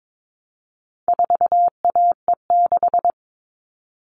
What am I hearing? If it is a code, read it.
Morse code, heard as a keyed tone, about 22 wpm, 700 Hz. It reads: 4AE6
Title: All Cut Numbers - 4-Digit 22wpm